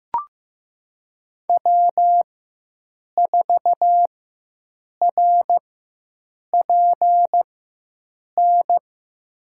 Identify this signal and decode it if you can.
Morse code, heard as a keyed tone, about 15 wpm, spaced as for 7 wpm, 700 Hz. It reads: W4RPN
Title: USA Call Signs - 5 Characters - 4x Character Spacing - 15wpm